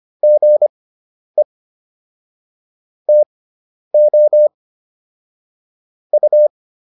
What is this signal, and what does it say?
Morse code, heard as a keyed tone, about 25 wpm, spaced as for 10 wpm, 600 Hz. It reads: GE TO U